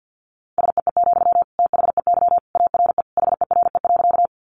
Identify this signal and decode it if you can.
Morse code, heard as a keyed tone, about 50 wpm, 700 Hz. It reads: SEE, THEY ARE HERE.
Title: Sentences from Top 200 Words 50wpm